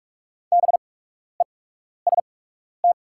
Morse code, heard as a keyed tone, about 46 wpm, spaced as for 12 wpm, 690 Hz. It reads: BEST
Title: 8x Character Spacing - Top 500 Words - 45wpm